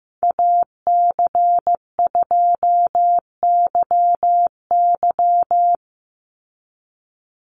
Morse code, heard as a keyed tone, about 15 wpm, 700 Hz. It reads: AC2YY
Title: USA Call Signs 15wpm